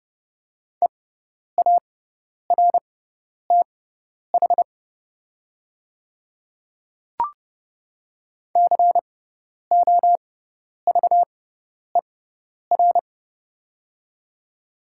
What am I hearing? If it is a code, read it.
Morse code, heard as a keyed tone, about 30 wpm, spaced as for 10 wpm, 700 Hz. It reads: EARTH COVER
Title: 6x Character Spacing - Top 300 Words - 30wpm